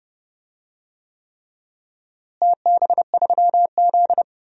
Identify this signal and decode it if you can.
Morse code, heard as a keyed tone, about 30 wpm, 700 Hz. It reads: TB3Z
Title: DXCC-100 Call Signs - 4 Characters - Repeated 3x - 30wpm